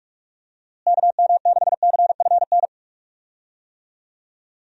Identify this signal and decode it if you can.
Morse code, heard as a keyed tone, about 47 wpm, 700 Hz. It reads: KM6CFN
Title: USA Call Signs - 6 Characters - Repeated 2x - 45wpm